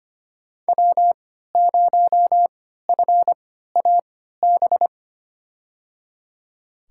Morse code, heard as a keyed tone, about 25 wpm, spaced as for 14 wpm, 700 Hz. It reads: W0FAB
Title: USA Call Signs - 5 Characters - 3x Character Spacing - 25wpm